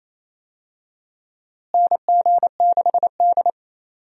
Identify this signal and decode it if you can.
Morse code, heard as a keyed tone, about 28 wpm, 700 Hz. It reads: NG6D
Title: USA Call Signs - 4 Characters - Repeated 3x - 28wpm